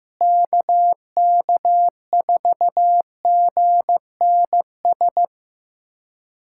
Morse code, heard as a keyed tone, about 15 wpm, 700 Hz. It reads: KK4GNS
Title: USA Callsign DE Callsign 15wpm